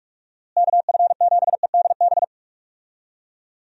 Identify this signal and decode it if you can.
Morse code, heard as a keyed tone, about 43 wpm, 700 Hz. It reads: KF7EDB
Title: USA Call Signs - 6 Characters - Repeated 2x - 45wpm